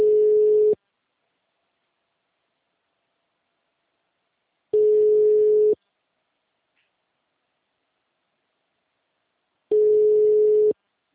Ringback tone (Europe, 425 Hz).